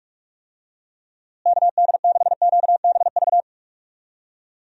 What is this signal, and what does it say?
Morse code, heard as a keyed tone, about 44 wpm, 700 Hz. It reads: KD6QBV
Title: USA Call Signs - 6 Characters - Repeated 2x - 45wpm